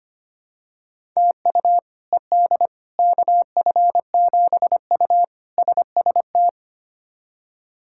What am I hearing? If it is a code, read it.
Morse code, heard as a keyed tone, about 25 wpm, 700 Hz. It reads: TU ED KF7U SST
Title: K1USN SST QSO Practice - 25wpm